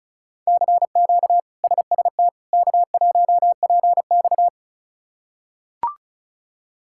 Morse code, heard as a keyed tone, about 34 wpm, 700 Hz. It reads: CQ SST K1PX
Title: K1USN SST QSO Practice - 35wpm